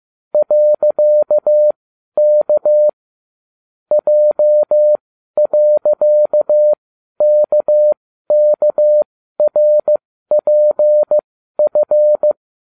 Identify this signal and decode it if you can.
Morse code, heard as a keyed tone, about 15 wpm, spaced as for 12 wpm, 600 Hz. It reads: .K J.KKRPF